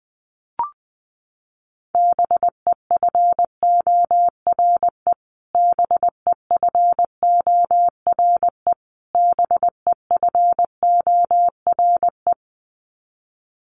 Morse code, heard as a keyed tone, about 20 wpm, 700 Hz. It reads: BEFORE BEFORE BEFORE
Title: Top 500 Words Repeated 20wpm